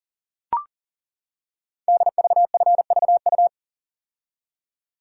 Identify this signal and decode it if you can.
Morse code, heard as a keyed tone, about 40 wpm, 700 Hz. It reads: DVFVU
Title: Code Groups - 5 Letters - 40wpm